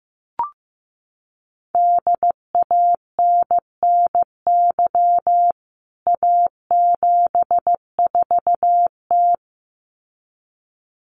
Morse code, heard as a keyed tone, about 15 wpm, 700 Hz. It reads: DANNY A74T
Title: CWT Exchange 15wpm